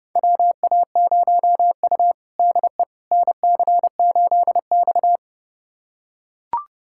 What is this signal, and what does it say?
Morse code, heard as a keyed tone, about 30 wpm, 700 Hz. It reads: WA0U DE NC8X
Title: USA Callsign DE Callsign - 4 Characters - 30wpm